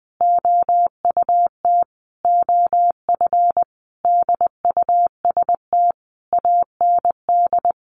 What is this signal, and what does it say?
Morse code, heard as a keyed tone, about 20 wpm, 700 Hz. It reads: OUT OF DUST AND
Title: A Sound Of Thunder 20wpm